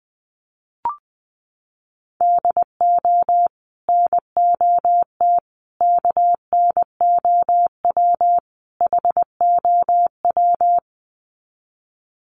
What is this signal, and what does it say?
Morse code, heard as a keyed tone, about 20 wpm, 700 Hz. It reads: DO NOT KNOW HOW
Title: Sets of 4 Words 20wpm